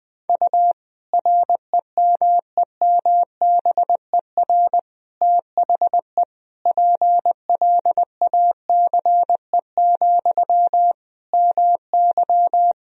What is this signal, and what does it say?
Morse code, heard as a keyed tone, about 21 wpm, 700 Hz. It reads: U REMEMBER THE PLACE, MY
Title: Sentences from Top 400 Words and Abbreviations 20wpm